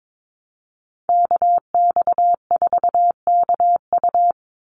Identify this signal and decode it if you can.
Morse code, heard as a keyed tone, about 22 wpm, 690 Hz. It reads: KX4KU
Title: Call Signs Repeated 2x - 22wpm